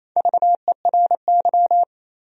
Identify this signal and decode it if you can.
Morse code, heard as a keyed tone, about 28 wpm, 700 Hz. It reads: VERY
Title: 3x Word-Spacing - Sentences from Top 200 Words - 28wpm